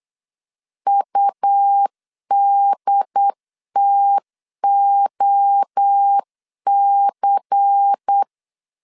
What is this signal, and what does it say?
Morse code, heard as a keyed tone, about 8 wpm, 800 Hz. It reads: UDTOC
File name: CW / DET / 1 6 1500C S C 8wpm 5219